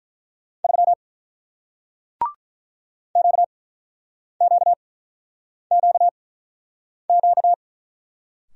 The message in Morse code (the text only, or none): 3 QQQQ